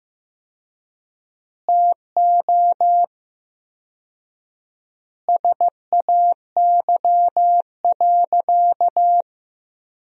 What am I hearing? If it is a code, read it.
Morse code, heard as a keyed tone, about 15 wpm, 700 Hz. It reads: TO SAY.
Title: 4x Word-Spacing - Sentences from Top 500 Words - 15wpm